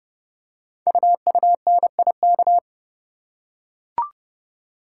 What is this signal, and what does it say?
Morse code, heard as a keyed tone, about 30 wpm, 700 Hz. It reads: UUNIK
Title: Code Groups - 5 Letters - 30wpm